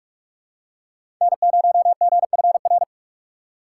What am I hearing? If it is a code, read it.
Morse code, heard as a keyed tone, about 43 wpm, 700 Hz. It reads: N0GFR
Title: USA Call Signs - 5 Characters - 3x Character Spacing - 45wpm